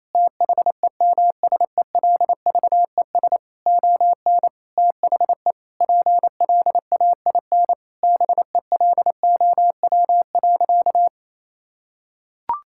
Morse code, heard as a keyed tone, about 28 wpm, 700 Hz. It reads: THEMSELVES ON THE PLAIN BELOW.
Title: Sentences from Top 1000 Words - Review Part 5 - 28wpm